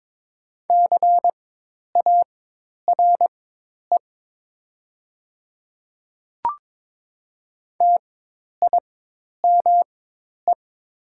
Morse code, heard as a keyed tone, about 21 wpm, spaced as for 10 wpm, 700 Hz. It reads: CARE TIME
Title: 4x Character Spacing - Top 500 Words - 22wpm